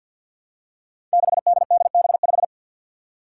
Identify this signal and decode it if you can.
Morse code, heard as a keyed tone, about 47 wpm, 700 Hz. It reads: 6DDB5
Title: Five Challenging Letter-Number Combinations 50wpm